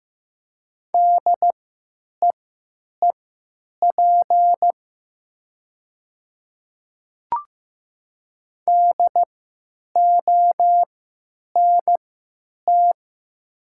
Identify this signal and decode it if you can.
Morse code, heard as a keyed tone, about 15 wpm, spaced as for 9 wpm, 700 Hz. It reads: DEEP DONT
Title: 3x Character Spacing - Top 500 Words - 15wpm